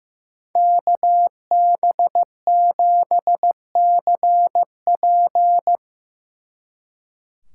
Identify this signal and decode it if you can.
Morse code, heard as a keyed tone, about 16 wpm, 700 Hz. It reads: KB7CP
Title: Speed Racing - USA 2x2 Call Signs 15wpm